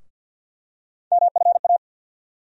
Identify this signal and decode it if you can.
Morse code, heard as a keyed tone, about 52 wpm, 700 Hz. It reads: MFA
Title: USA Call Sign Postfixes - 3 Letters 50wpm